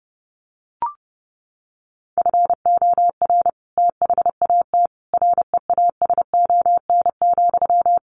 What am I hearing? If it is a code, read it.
Morse code, heard as a keyed tone, about 30 wpm, 700 Hz. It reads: FOR THAT REASON,
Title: Sentences from Top 1500 Words 30wpm